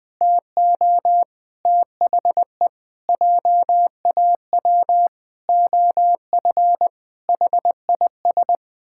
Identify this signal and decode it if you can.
Morse code, heard as a keyed tone, about 20 wpm, 700 Hz. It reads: TO THE JAW OF HIS
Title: Pangram Challenge v2 - 20wpm